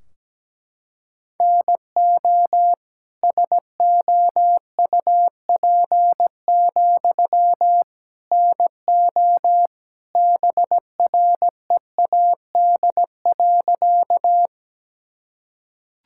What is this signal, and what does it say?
Morse code, heard as a keyed tone, about 17 wpm, 700 Hz. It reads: NO SOUP, NO BREAD.
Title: Sentences from Top 1600 Words 17wpm